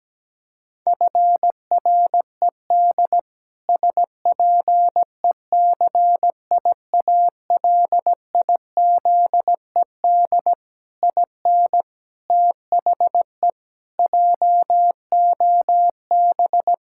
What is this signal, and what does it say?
Morse code, heard as a keyed tone, about 17 wpm, 700 Hz. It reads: FRED SPECIALIZED IN THE JOB
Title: Pangram Challenge v2 - 17wpm